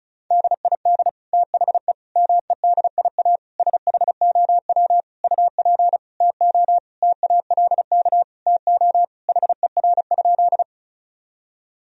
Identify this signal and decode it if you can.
Morse code, heard as a keyed tone, about 36 wpm, 700 Hz. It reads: DID THE MEDIA SHOW UP TO TALK TO HER?
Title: Sentences from Top 1000 Words - Review Part 4 - 35wpm